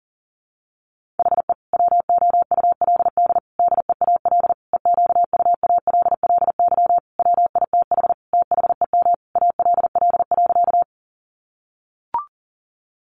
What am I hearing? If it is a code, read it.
Morse code, heard as a keyed tone, about 40 wpm, 700 Hz. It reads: HE WOULD DEAL EQUALLY WITH THEM ALL.
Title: Sentences from Top 1400 Words 40wpm